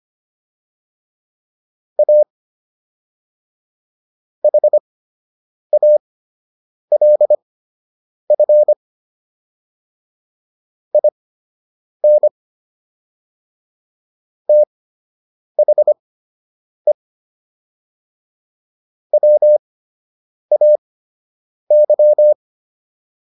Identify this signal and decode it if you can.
Morse code, heard as a keyed tone, about 25 wpm, spaced as for 8 wpm, 600 Hz. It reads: A HALF IN THE WAY